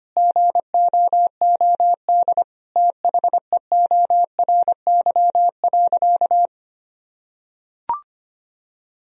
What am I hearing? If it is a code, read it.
Morse code, heard as a keyed tone, about 25 wpm, 700 Hz. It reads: GOOD THEORY.
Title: Sentences from Top 1000 Words - Review Part 3 - 25wpm